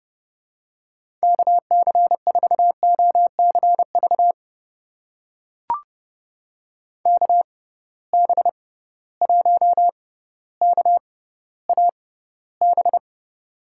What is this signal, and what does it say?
Morse code, heard as a keyed tone, about 30 wpm, spaced as for 15 wpm, 700 Hz. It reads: KC4OCV KB1KAB
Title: USA Call Signs - 6 Characters - 6x Character Spacing - 30wpm